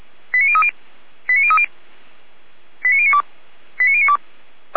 Five-tone selective call (ZVEI), heard as quick runs of five tones, tones of about 80 ms. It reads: 89030 89030 89092 89092